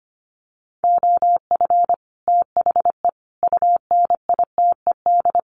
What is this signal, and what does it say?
Morse code, heard as a keyed tone, about 25 wpm, 700 Hz. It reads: OF THE UNITED
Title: Sets of 4 Words 25wpm